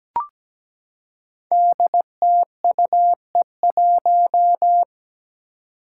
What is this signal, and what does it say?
Morse code, heard as a keyed tone, about 17 wpm, 700 Hz. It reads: DTUE1
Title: Five Challenging Letter-Number Combinations 17wpm